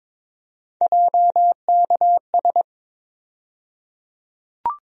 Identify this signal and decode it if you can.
Morse code, heard as a keyed tone, about 22 wpm, 700 Hz. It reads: JKS
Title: USA Call Sign Postfixes - 3 Letters 22wpm